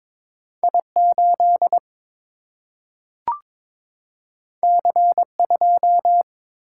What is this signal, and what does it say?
Morse code, heard as a keyed tone, about 22 wpm, 700 Hz. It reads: I 8 C 2
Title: Single-Letter Single-Number 22wpm